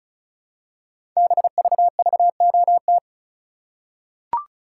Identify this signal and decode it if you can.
Morse code, heard as a keyed tone, about 36 wpm, 700 Hz. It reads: BVVOT
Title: Code Groups - 5 Letter:Numbers - 35wpm